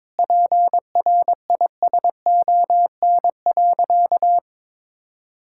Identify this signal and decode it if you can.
Morse code, heard as a keyed tone, about 22 wpm, 700 Hz. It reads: PRISON.
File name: practice / 22wpm / Sentences with 8-Word Phrases - 22wpm